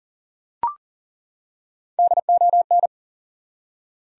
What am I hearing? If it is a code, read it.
Morse code, heard as a keyed tone, about 40 wpm, 700 Hz. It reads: DON
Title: USA Common Names - Max 3 Letters - 40wpm